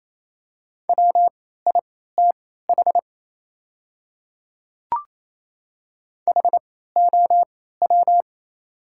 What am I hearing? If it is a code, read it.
Morse code, heard as a keyed tone, about 28 wpm, spaced as for 16 wpm, 700 Hz. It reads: WITH HOW
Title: 3x Character Spacing - Top 100 Words - 28wpm